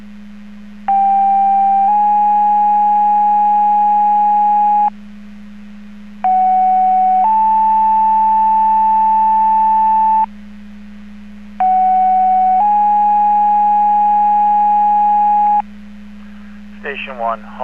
Two-tone page, heard as a one-second tone, then a three-second tone, about 790 then 830 Hz, about 750 then 880 Hz, about 750 then 830 Hz. It is heard from 0.9 to 15.6 s.